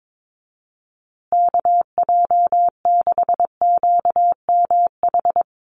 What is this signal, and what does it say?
Morse code, heard as a keyed tone, about 22 wpm, 700 Hz. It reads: KJ6QMH
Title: USA Call Signs Repeated 2x 22wpm